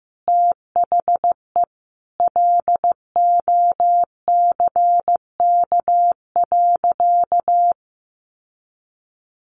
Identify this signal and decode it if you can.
Morse code, heard as a keyed tone, about 15 wpm, 700 Hz. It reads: THE LOCK.
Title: Sentences from Top 1300 Words 15wpm